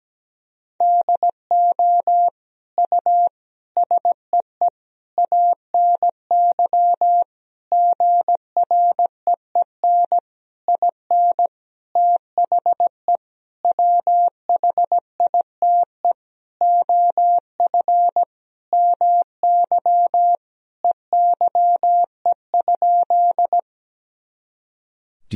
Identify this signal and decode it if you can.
Morse code, heard as a keyed tone, about 17 wpm, 700 Hz. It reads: DO U SEE ANY GREEN IN THE WHITE OF MY EYE?
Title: Sentences from Top 500 Words and Abbreviations 17wpm